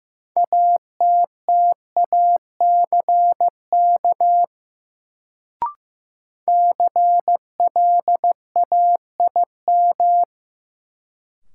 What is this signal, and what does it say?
Morse code, heard as a keyed tone, about 15 wpm, 700 Hz. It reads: ATTACK CLAIM
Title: Sentences from Top 1500 Words - New Word Drill - 15wpm